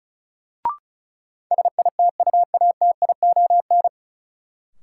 Morse code, heard as a keyed tone, about 34 wpm, 700 Hz. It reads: SITUATION
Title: Sentences from Top 1200 Words - New Word Drill - 35wpm